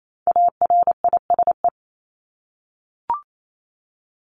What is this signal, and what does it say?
Morse code, heard as a keyed tone, about 28 wpm, 700 Hz. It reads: ARISE
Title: Top 1501-1750 Words 28wpm